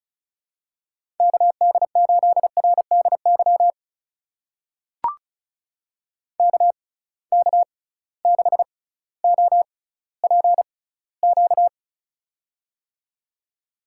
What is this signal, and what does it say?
Morse code, heard as a keyed tone, about 36 wpm, spaced as for 18 wpm, 700 Hz. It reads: KD8RDY KK6OPQ
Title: USA Call Signs - 6 Characters - 6x Character Spacing - 35wpm